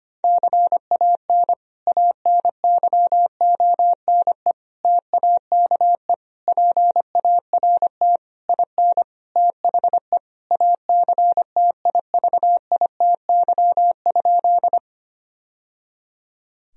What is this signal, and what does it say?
Morse code, heard as a keyed tone, about 25 wpm, 700 Hz. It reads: CAN ANYONE TAKE PART IN THE ACTIVITY?
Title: Sentences from Top 1000 Words - Review Part 4 - 25wpm